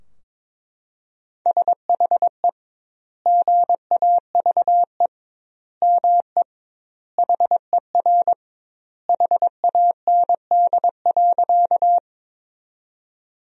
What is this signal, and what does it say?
Morse code, heard as a keyed tone, about 22 wpm, 700 Hz. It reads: SHE GAVE ME HER HAND.
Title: 2x Word-Spacing - Sentences from Top 500 Words - 22wpm